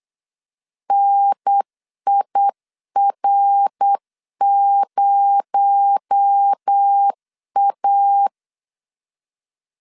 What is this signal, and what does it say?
Morse code, heard as a keyed tone, about 8 wpm, 800 Hz. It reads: NIR0A
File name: LEZIONI / 9 28 8wpm 3512